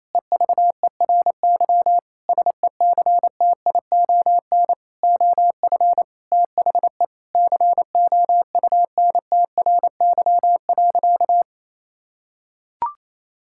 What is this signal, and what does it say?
Morse code, heard as a keyed tone, about 28 wpm, 700 Hz. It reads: EVERY SECTION OF THE COUNTRY.
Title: Sentences from Top 1000 Words - Review Part 5 - 28wpm